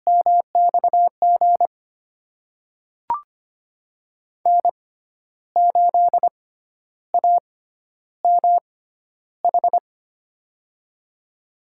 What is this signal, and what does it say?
Morse code, heard as a keyed tone, about 25 wpm, spaced as for 11 wpm, 700 Hz. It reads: MXG N8AMH
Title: USA Call Signs - 5 Characters - 6x Character Spacing - 25wpm